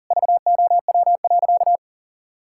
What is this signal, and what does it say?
Morse code, heard as a keyed tone, about 37 wpm, 700 Hz. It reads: VOW.
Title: Pangram Challenge v2 - 40wpm